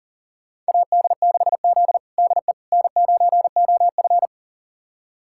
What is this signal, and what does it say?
Morse code, heard as a keyed tone, about 37 wpm, 700 Hz. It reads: AD6Z DE N9OF
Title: USA Callsign DE Callsign - 4 Characters - 40wpm